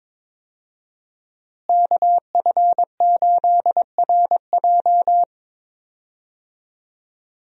Morse code, heard as a keyed tone, about 22 wpm, 700 Hz. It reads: KF8RJ
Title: USA Call Signs - 5 Characters - Repeated 3x - 22wpm